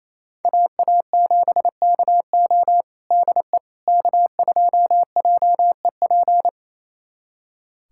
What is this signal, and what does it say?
Morse code, heard as a keyed tone, about 27 wpm, 700 Hz. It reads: AA7KO DE K2JEP
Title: USA Callsign DE Callsign - 5 Characters - 28wpm